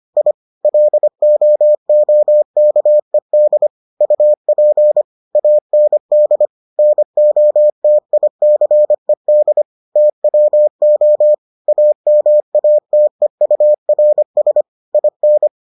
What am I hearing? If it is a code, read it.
Morse code, heard as a keyed tone, about 25 wpm, 600 Hz. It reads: I LOOKED UP AND NOTICED TWO AMATEURS IN